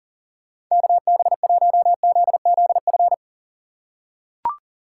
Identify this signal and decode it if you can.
Morse code, heard as a keyed tone, about 40 wpm, 700 Hz. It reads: KB1ZZF
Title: USA Call Signs - 4x Character Spacing - 40wpm